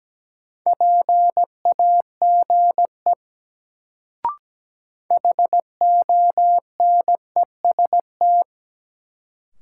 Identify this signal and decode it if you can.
Morse code, heard as a keyed tone, about 17 wpm, 700 Hz. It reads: PAGE HONEST
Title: Sets of 5 Words Encore - New Word Drill - 17wpm